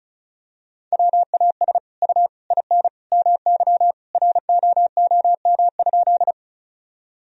Morse code, heard as a keyed tone, about 36 wpm, 700 Hz. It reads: WAS U IN MY ROOM?
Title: Sentences from Top 300 Words and Abbreviations 35wpm